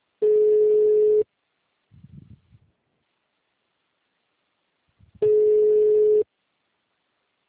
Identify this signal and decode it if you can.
Ringback tone (Europe, 425 Hz).
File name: out- -2031- - - .0